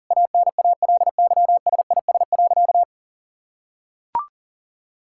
Morse code, heard as a keyed tone, about 40 wpm, 700 Hz. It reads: ANALYSIS.
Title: Sentences from Top 1000 Words - Review Part 5 - 40wpm